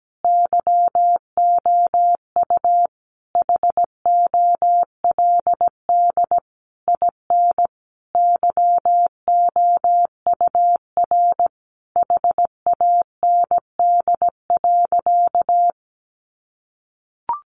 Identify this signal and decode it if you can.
Morse code, heard as a keyed tone, about 17 wpm, 700 Hz. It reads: YOU HOLD IN YOUR HAND.
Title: Sentences from Top 400 Words 17wpm